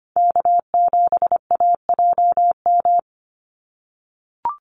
Morse code, heard as a keyed tone, about 25 wpm, 700 Hz. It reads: K7AJM
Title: Call Signs - 25wpm